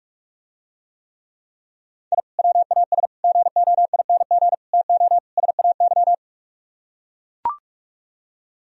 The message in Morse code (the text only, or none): I WAS GOING TO SAY